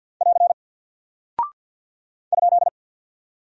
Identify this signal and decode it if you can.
Morse code, heard as a keyed tone, about 48 wpm, 700 Hz. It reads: <AR> ?
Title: Single Character (Letter:Number:Punctuation:Prosign) Alt 50wpm